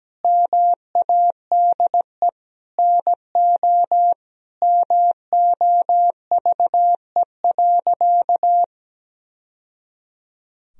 Morse code, heard as a keyed tone, about 17 wpm, 700 Hz. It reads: MADE NO MOVE.
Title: Sentences from Top 1000 Words - Review Part 1 - 17wpm